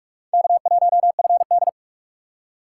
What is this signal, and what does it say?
Morse code, heard as a keyed tone, about 40 wpm, 700 Hz. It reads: K1FD